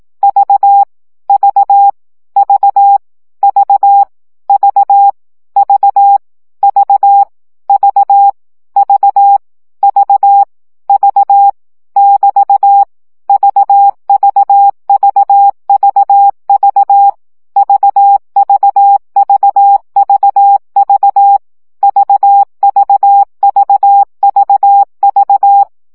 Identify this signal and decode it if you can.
Morse code, heard as a keyed tone, about 18 wpm, 800 Hz. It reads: V V V V V V V V V V V <BT> VVVVV VVVVV VVVVV
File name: CW gyakorlas v 15 baud